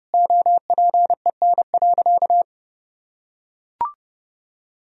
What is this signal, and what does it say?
Morse code, heard as a keyed tone, about 32 wpm, 700 Hz. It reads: OPEN.
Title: Sentences from Top 1000 Words - Review Part 3 - 30wpm